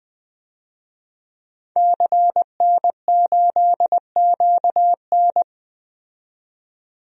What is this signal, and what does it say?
Morse code, heard as a keyed tone, about 20 wpm, 700 Hz. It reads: CN8QN
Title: DXCC-100 Call Signs - 5 Characters - 20wpm